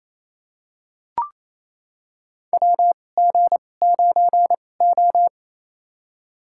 Morse code, heard as a keyed tone, about 28 wpm, spaced as for 20 wpm, 700 Hz. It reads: WG9O
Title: USA Call Signs - 2x Character Spacing - 28wpm